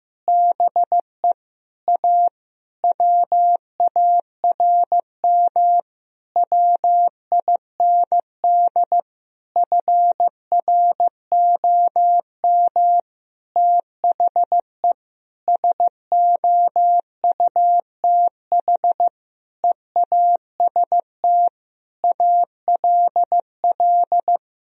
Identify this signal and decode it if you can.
Morse code, heard as a keyed tone, about 15 wpm, 700 Hz. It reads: BE A WARM WIND FROM THE SOUTH EAST ALL
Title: Sentences from Top 1000 Words - Review Part 3 - 15wpm